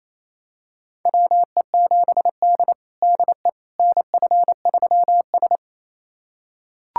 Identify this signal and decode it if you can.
Morse code, heard as a keyed tone, about 28 wpm, 700 Hz. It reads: WE7D DE NF3S
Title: USA Callsign DE Callsign - 4 Characters - 28wpm